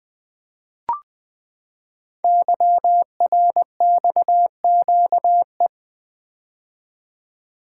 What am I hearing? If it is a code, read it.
Morse code, heard as a keyed tone, about 20 wpm, 700 Hz. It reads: YRXQE
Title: Code Groups - 5 Letters - 20wpm